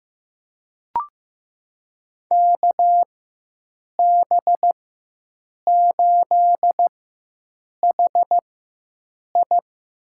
Morse code, heard as a keyed tone, about 15 wpm, spaced as for 7 wpm, 700 Hz. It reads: KB8HI
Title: USA Call Signs - 4x Character Spacing - 15wpm